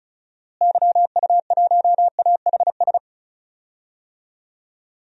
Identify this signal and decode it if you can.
Morse code, heard as a keyed tone, about 36 wpm, 700 Hz. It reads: YU1AHS
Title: DXCC-100 Call Signs - 6 Characters - Repeated 3x - 35wpm